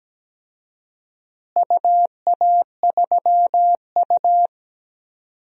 Morse code, heard as a keyed tone, about 17 wpm, 700 Hz. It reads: UA3U
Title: DXCC-100 Call Signs - 4 Characters - Repeated 2x - 17wpm